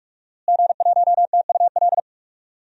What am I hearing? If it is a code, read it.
Morse code, heard as a keyed tone, about 42 wpm, 700 Hz. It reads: G1TUL
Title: DXCC-100 Call Signs - 5 Characters - Repeated 3x - 45wpm